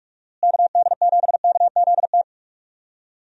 Morse code, heard as a keyed tone, about 43 wpm, 700 Hz. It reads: KD7KZT
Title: USA Call Signs - 6 Characters - Repeated 2x - 45wpm